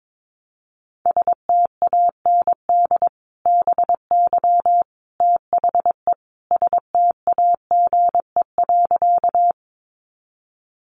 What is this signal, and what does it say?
Morse code, heard as a keyed tone, about 22 wpm, 700 Hz. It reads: STAND BY THE STAGE.
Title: Sentences from Top 1200 Words 22wpm